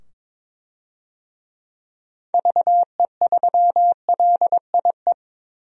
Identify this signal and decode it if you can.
Morse code, heard as a keyed tone, about 22 wpm, 700 Hz. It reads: VE3LIE
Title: DXCC-100 Call Signs - 6 Characters - 22wpm